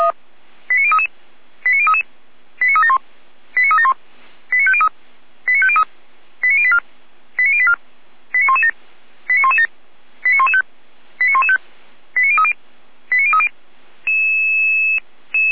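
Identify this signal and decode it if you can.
Five-tone selective call (ZVEI), heard as quick runs of five tones, tones of about 70 ms. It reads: 89033 89033 89371 89371 89503 89503 89084 89084 89117 89117 89115 89115 89030 89030
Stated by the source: ZVEI five-tone selective call received by SDR.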